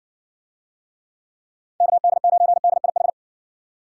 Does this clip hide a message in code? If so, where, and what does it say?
Yes: Morse code from 1.8 to 3.1 s, heard as a keyed tone, about 60 wpm, 700 Hz. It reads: KD9DIH